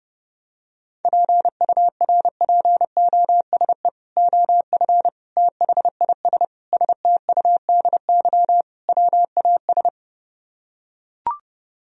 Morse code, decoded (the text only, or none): PURPOSE OF THIS STUDY WAS